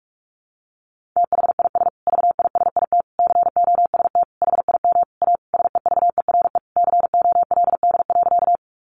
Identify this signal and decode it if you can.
Morse code, heard as a keyed tone, about 44 wpm, 700 Hz. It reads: THIS VISIT COST HIM A SEVERE COLD.
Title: Sentences from Top 1400 Words 45wpm